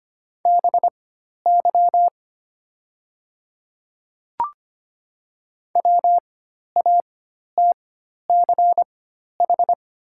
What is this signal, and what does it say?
Morse code, heard as a keyed tone, about 25 wpm, spaced as for 12 wpm, 700 Hz. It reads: BY WATCH
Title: 4x Character Spacing - Top 300 Words - 25wpm